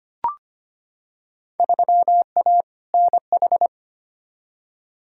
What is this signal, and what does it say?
Morse code, heard as a keyed tone, about 25 wpm, 700 Hz. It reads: 3A NH
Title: Field Day Exchanges Encore - 25wpm